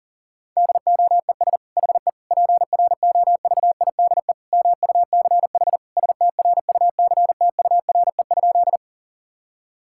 Morse code, heard as a keyed tone, about 40 wpm, 700 Hz. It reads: DOES HE PROVIDE MUCH STRUCTURE?
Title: Sentences from Top 1000 Words - Review Part 6 - 40wpm